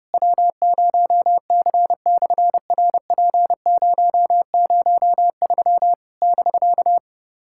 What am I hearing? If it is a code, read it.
Morse code, heard as a keyed tone, about 30 wpm, 700 Hz. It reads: W0C/RP003 <BK>
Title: SOTA QSO Practice 30wpm